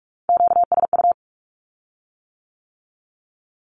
Morse code, heard as a keyed tone, about 46 wpm, 700 Hz. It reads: QSU
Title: Q-Codes with Spoken Meaning - 45wpm